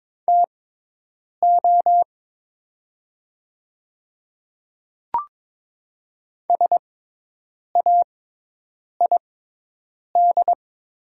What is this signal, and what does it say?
Morse code, heard as a keyed tone, about 22 wpm, spaced as for 8 wpm, 700 Hz. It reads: TO SAID
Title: 6x Character Spacing - Top 100 Words - 22wpm